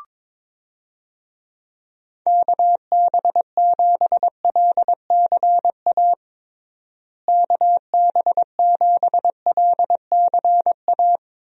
Morse code, heard as a keyed tone, about 21 wpm, 700 Hz. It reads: KB7LCA KB7LCA
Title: USA Call Signs - 6 Characters - Repeated 2x - 22wpm